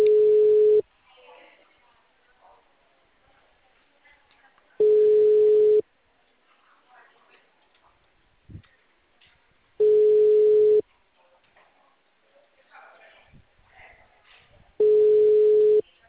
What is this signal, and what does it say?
Ringback tone (Europe, 425 Hz), with speech.